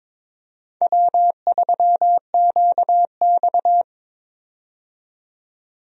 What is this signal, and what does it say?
Morse code, heard as a keyed tone, about 22 wpm, 700 Hz. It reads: W3QX
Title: USA Call Signs - 4 Characters - 22wpm